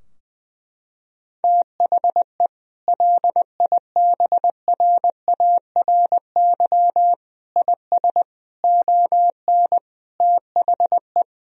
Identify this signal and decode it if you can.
Morse code, heard as a keyed tone, about 21 wpm, 700 Hz. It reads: THE LIBRARY IS ON THE
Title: Sentences from Top 1000 Words - Review Part 4 - 20wpm